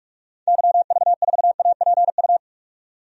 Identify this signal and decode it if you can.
Morse code, heard as a keyed tone, about 47 wpm, 700 Hz. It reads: YV4APU
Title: DXCC-100 Call Signs - 6 Characters - 45wpm